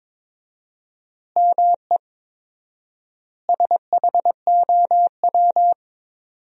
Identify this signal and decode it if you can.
Morse code, heard as a keyed tone, about 22 wpm, 700 Hz. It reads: ME SHOW
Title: 4x Word-Spacing - Sentences from Top 300 Words - 22wpm